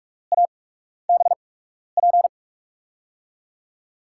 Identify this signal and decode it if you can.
Morse code, heard as a keyed tone, about 45 wpm, spaced as for 12 wpm, 700 Hz. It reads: ABP
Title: USA Call Signs - 6 Characters - 8x Character Spacing - 45wpm